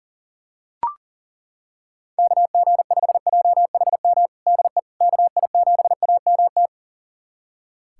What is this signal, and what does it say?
Morse code, heard as a keyed tone, about 40 wpm, 700 Hz. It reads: KG5JHM DE KI7AMT
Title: USA Callsign DE Callsign - 6 Characters - 40wpm